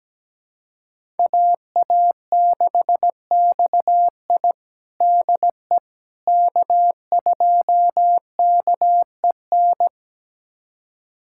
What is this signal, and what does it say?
Morse code, heard as a keyed tone, about 17 wpm, 700 Hz. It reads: AA6XI DE K2KEN
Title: USA Callsign DE Callsign - 5 Characters - 17wpm